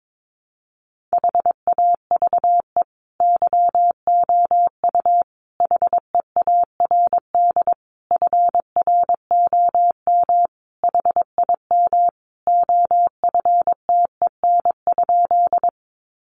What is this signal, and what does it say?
Morse code, heard as a keyed tone, about 22 wpm, 700 Hz. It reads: HAVE YOU HEARD FROM HIM OFTEN?
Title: Sentences from Top 400 Words 22wpm